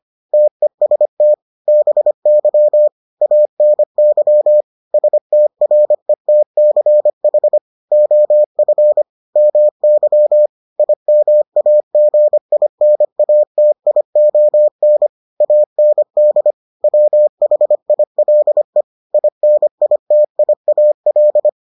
Morse code, heard as a keyed tone, about 25 wpm, 600 Hz. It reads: TEST BY ANY STRETCH OF MY IMAGINATION AND WHILE INITIAL